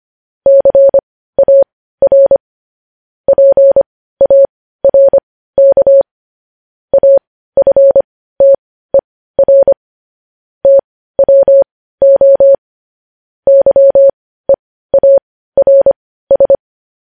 Morse code, heard as a keyed tone, about 25 wpm, spaced as for 15 wpm, 550 Hz. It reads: CAR PARK AFTER TWO YEARS
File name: podcast / News headlines-1x-25x15